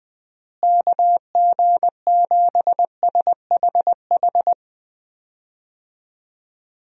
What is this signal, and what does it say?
Morse code, heard as a keyed tone, about 20 wpm, 700 Hz. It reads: KG7SHH
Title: USA Call Signs - 6 Characters - Repeated 3x - 20wpm